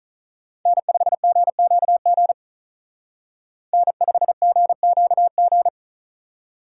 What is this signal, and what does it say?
Morse code, heard as a keyed tone, about 43 wpm, 700 Hz. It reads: N5GQG N5GQG
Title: Speed Racing - USA 1x3 Call Signs 35wpm